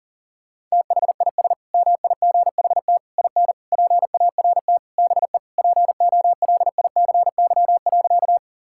Morse code, heard as a keyed tone, about 40 wpm, 700 Hz. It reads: THIS MIGHT IN PART BE POLICY.
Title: Sentences from Top 1000 Words - Review Part 5 - 40wpm